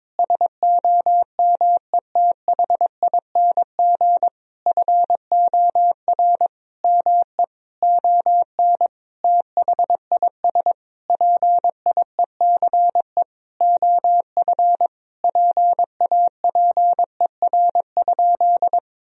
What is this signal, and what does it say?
Morse code, heard as a keyed tone, about 22 wpm, 700 Hz. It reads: SOMETHING FOR ME ON THIS PIECE OF PAPER?
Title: Sentences from Top 1000 Words - Review Part 2 - 22wpm